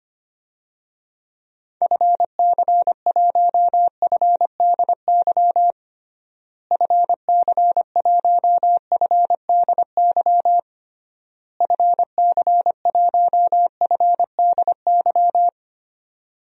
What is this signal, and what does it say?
Morse code, heard as a keyed tone, about 25 wpm, 700 Hz. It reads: FC1FDY FC1FDY FC1FDY
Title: DXCC-100 Call Signs - 6 Characters - Repeated 3x - 25wpm